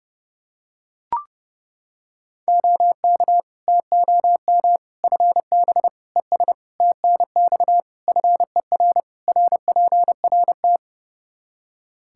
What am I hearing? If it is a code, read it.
Morse code, heard as a keyed tone, about 30 wpm, 700 Hz. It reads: OK TOM FB ES TNX FER RPRT
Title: Practice QSO - Exchange 2 - Info - 30wpm